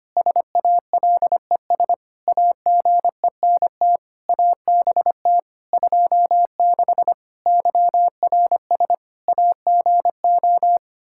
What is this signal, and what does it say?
Morse code, heard as a keyed tone, about 25 wpm, 700 Hz. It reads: SALES AGENT ABT 26 YRS AGO
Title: Practice QSO - Exchange 3 - More Info - 25wpm